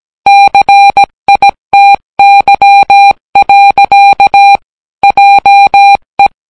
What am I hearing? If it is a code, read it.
Morse code, heard as a keyed tone, about 17 wpm, 800 Hz. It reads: CITY. JE